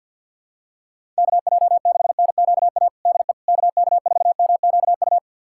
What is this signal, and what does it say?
Morse code, heard as a keyed tone, about 52 wpm, 700 Hz. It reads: KJ6NQA DE KK4MQU
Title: USA Callsign DE Callsign 50wpm